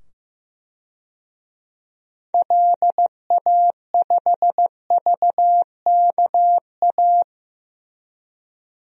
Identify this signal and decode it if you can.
Morse code, heard as a keyed tone, about 15 wpm, 700 Hz. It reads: LA5VKA
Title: DXCC-100 Call Signs - 6 Characters - 15wpm